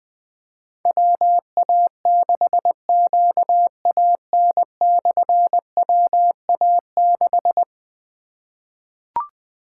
Morse code, heard as a keyed tone, about 20 wpm, 700 Hz. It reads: WA6QAN/WA6
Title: Difficult Call Signs 20wpm